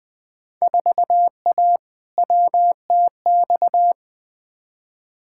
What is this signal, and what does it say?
Morse code, heard as a keyed tone, about 20 wpm, 700 Hz. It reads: 4A WTX
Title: Field Day Exchanges - 20wpm